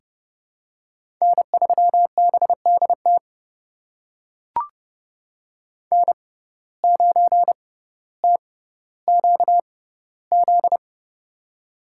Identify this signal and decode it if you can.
Morse code, heard as a keyed tone, about 30 wpm, spaced as for 15 wpm, 700 Hz. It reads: N3BDT N9TQZ
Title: USA Call Signs - 5 Characters - 6x Character Spacing - 30wpm